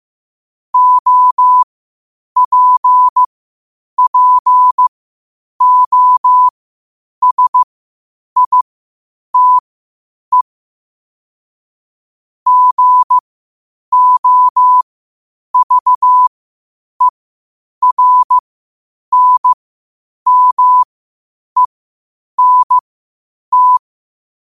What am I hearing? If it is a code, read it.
Morse code, heard as a keyed tone, about 15 wpm, spaced as for 8 wpm, 1000 Hz. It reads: OPPOSITE GOVERNMENT
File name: qotdfiles / 8wpm